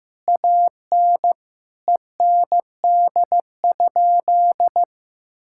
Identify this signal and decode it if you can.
Morse code, heard as a keyed tone, about 15 wpm, 700 Hz. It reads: AN END?
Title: Sentences from Top 1000 Words - Review Part 5 - 15wpm